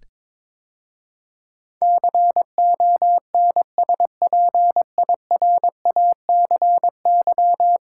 Morse code, heard as a keyed tone, about 22 wpm, 700 Hz. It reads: CONSPIRACY
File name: practice / 22wpm / 10 Letter Words - First 3 Letters Given - 22wpm